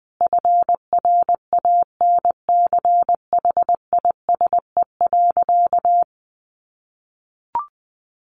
Morse code, heard as a keyed tone, about 20 wpm, 700 Hz. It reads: FRANCHISE.
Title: A Sound Of Thunder 20wpm